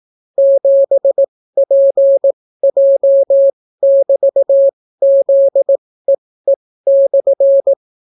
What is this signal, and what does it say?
Morse code, heard as a keyed tone, about 18 wpm, spaced as for 14 wpm, 550 Hz. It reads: 7PJ<BT>ZEE/